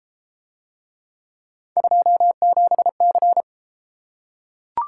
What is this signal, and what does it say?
Morse code, heard as a keyed tone, about 31 wpm, 700 Hz. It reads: 27C